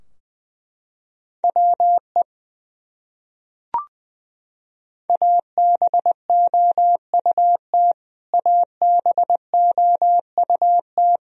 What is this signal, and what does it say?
Morse code, heard as a keyed tone, about 20 wpm, 700 Hz. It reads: WE ABOUT ABOUT
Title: Top 100 Words Repeated 20wpm